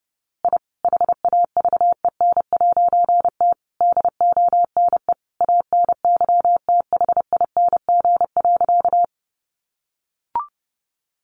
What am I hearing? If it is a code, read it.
Morse code, heard as a keyed tone, about 30 wpm, 700 Hz. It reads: I HAVEN'T DONE ANYTHING.
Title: A Sound Of Thunder 30wpm